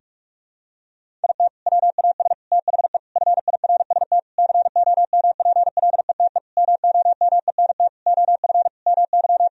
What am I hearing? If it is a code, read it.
Morse code, heard as a keyed tone, about 43 wpm, 700 Hz. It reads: IT WAS THE FIRST COMPLETE MOMENT OF MY
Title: Sentences from Top 1000 Words - Review Part 2 - 45wpm